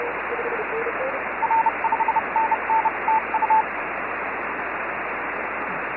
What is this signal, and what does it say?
Morse code, heard as a keyed tone, about 30 wpm, 880 Hz, with radio noise. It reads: R5NN TU
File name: FO29 JD1BLY